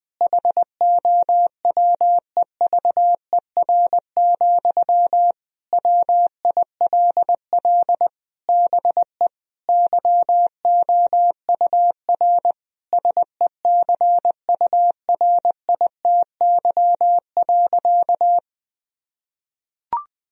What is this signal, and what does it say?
Morse code, heard as a keyed tone, about 20 wpm, 700 Hz. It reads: HOWEVER, WILL BE YOUR SECURITY.
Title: Sentences from Top 1000 Words - Review Part 4 - 20wpm